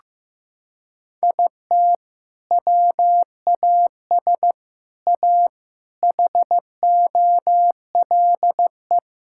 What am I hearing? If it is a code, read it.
Morse code, heard as a keyed tone, about 16 wpm, 700 Hz. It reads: IT WAS A HOLE
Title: Sentences from Top 1600 Words 15wpm